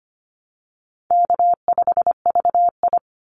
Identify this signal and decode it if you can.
Morse code, heard as a keyed tone, about 25 wpm, 700 Hz. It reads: K5VI
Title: Easy Call Signs Repeated 3x - 25wpm